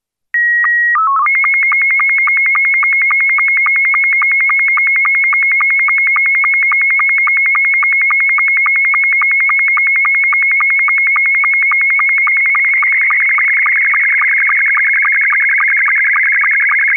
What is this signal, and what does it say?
SSTV (slow-scan television), Scottie 2.
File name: sstv-essexham-image01-scottie2